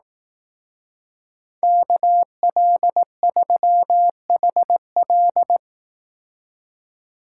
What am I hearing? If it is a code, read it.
Morse code, heard as a keyed tone, about 18 wpm, 700 Hz. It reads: KL3HL
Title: Speed Racing - USA 2x2 Call Signs 15wpm